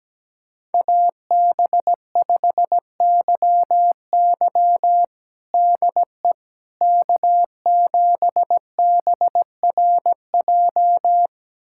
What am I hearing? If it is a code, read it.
Morse code, heard as a keyed tone, about 17 wpm, 700 Hz. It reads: AB5YY DE K7BRJ
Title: USA Callsign DE Callsign - 5 Characters - 17wpm